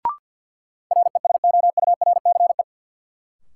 Morse code, heard as a keyed tone, about 53 wpm, 700 Hz. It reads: RESOURCE
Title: Sentences from Top 1300 Words - New Word Drill Alt - 50wpm